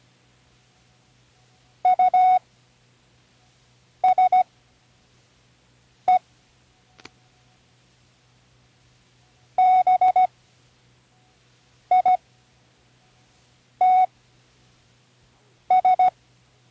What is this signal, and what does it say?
Morse code, heard as a keyed tone, about 16 wpm, spaced as for 5 wpm, 740 Hz. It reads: USE BITS
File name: Learning Morse Code (from Tune in the World)